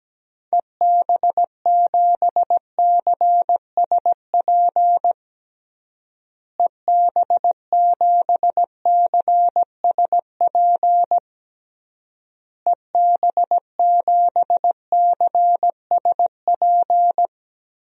Morse code, heard as a keyed tone, about 17 wpm, 700 Hz. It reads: EB7CSP EB7CSP EB7CSP
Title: DXCC-100 Call Signs - 6 Characters - Repeated 3x - 17wpm